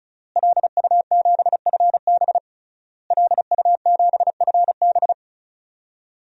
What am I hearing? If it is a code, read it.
Morse code, heard as a keyed tone, about 36 wpm, 700 Hz. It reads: LU7FB LU7FB
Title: DXCC-100 Call Signs - 5 Characters - Repeated 3x - 35wpm